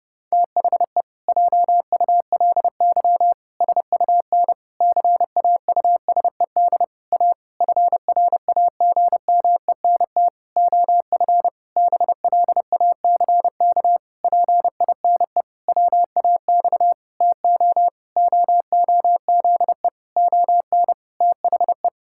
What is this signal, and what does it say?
Morse code, heard as a keyed tone, about 32 wpm, 700 Hz. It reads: THE JULY SUN CAUSED A FRAGMENT OF BLACK PINE WAX TO OOZE ON THE
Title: Pangram Challenge v2 - 30wpm